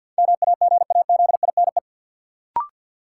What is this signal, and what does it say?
Morse code, heard as a keyed tone, about 53 wpm, 700 Hz. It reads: MAGAZINE
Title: Sentences from Top 1000 Words - Review Part 7 - New Word Drill - 50wpm